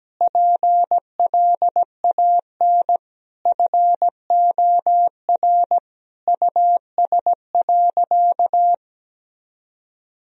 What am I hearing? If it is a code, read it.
Morse code, heard as a keyed tone, about 17 wpm, 700 Hz. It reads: PLAN FOR US.
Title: Sentences from Top 1000 Words - Review Part 2 - 17wpm